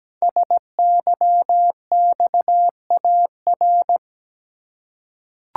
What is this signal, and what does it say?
Morse code, heard as a keyed tone, about 17 wpm, 700 Hz. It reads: SYXAR
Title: Code Groups - 5 Letters - 17wpm